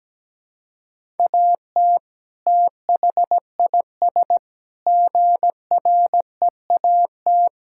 Morse code, heard as a keyed tone, about 17 wpm, 700 Hz. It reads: AT THIS GREAT
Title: Sentences from Top 1600 Words 17wpm